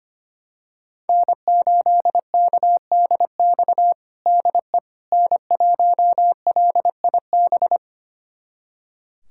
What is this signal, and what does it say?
Morse code, heard as a keyed tone, about 25 wpm, 700 Hz. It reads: N8KDX DE N1LIB
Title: USA Callsign DE Callsign - 5 Characters - 25wpm